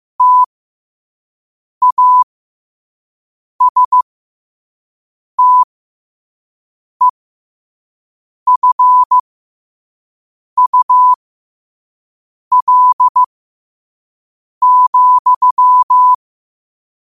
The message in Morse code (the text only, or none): TASTEFUL,